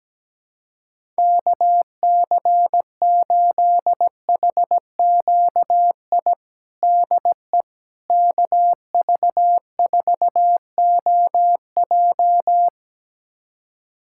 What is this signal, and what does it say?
Morse code, heard as a keyed tone, about 17 wpm, 700 Hz. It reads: KC8HQI DE KV4OJ
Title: USA Callsign DE Callsign 17wpm